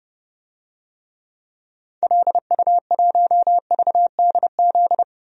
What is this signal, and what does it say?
Morse code, heard as a keyed tone, about 32 wpm, 700 Hz. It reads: LU1VDZ
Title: DXCC-100 Call Signs - 6 Characters - 30wpm